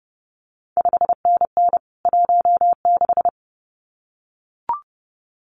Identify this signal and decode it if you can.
Morse code, heard as a keyed tone, about 30 wpm, 700 Hz. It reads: 5NN 16
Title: CQ WW DX Contest Exchange - 30wpm